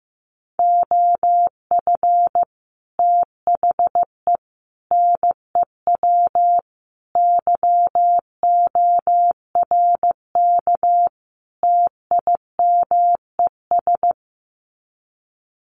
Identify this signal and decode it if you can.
Morse code, heard as a keyed tone, about 15 wpm, 700 Hz. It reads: OF THE NEW YORK TIMES
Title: Sets of 5 Words 15wpm